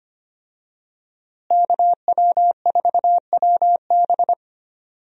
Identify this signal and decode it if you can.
Morse code, heard as a keyed tone, about 25 wpm, 700 Hz. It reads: KW4WB
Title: USA Call Signs - 5 Characters - Repeated 2x - 25wpm